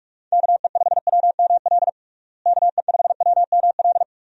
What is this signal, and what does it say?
Morse code, heard as a keyed tone, about 46 wpm, 700 Hz. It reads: KE5WML KE5WML
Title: USA Call Signs - 6 Characters - Repeated 2x - 45wpm